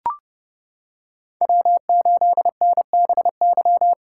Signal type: Morse code

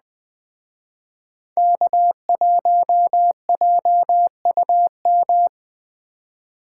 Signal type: Morse code

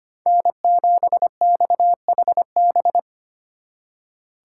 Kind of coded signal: Morse code

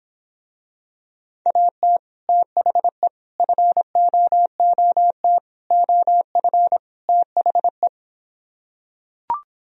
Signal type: Morse code

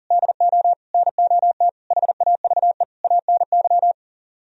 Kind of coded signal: Morse code

